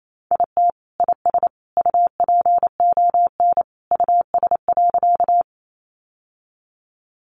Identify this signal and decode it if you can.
Morse code, heard as a keyed tone, about 28 wpm, 700 Hz. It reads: IT IS UPON US.
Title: Sentences from Top 1100 Words 28wpm